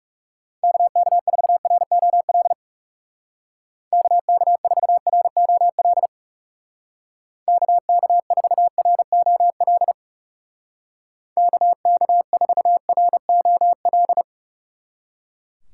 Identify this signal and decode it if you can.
Morse code, heard as a keyed tone, about 39 wpm, 700 Hz. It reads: KK4ROL KK4ROL KK4ROL KK4ROL